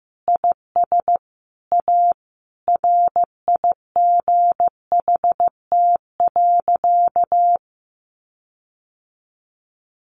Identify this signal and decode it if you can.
Morse code, heard as a keyed tone, about 15 wpm, 700 Hz. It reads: IS A RIGHT.
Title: Sentences from Top 1400 Words 15wpm